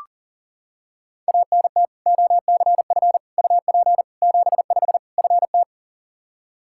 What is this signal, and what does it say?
Morse code, heard as a keyed tone, about 40 wpm, 700 Hz. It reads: ANT OCF UP 75 FT
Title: Practice QSO - Exchange 2 - Info - 40wpm